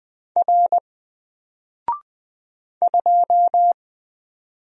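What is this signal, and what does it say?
Morse code, heard as a keyed tone, about 20 wpm, 700 Hz. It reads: R 2
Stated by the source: Single letter and number Morse drill at 20 wpm, from N4LCD.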